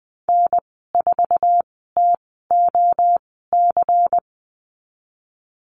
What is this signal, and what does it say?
Morse code, heard as a keyed tone, about 20 wpm, spaced as for 15 wpm, 700 Hz. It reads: N4TOC
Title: 173 Lesson 28g - Number 9 - Call Signs 20wpm